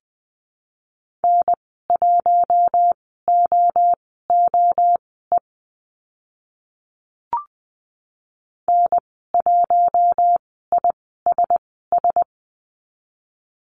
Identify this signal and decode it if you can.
Morse code, heard as a keyed tone, about 20 wpm, spaced as for 14 wpm, 700 Hz. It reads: N1OOE N1ISS
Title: 042 Lesson 8g - Number 1 - Call Signs 20wpm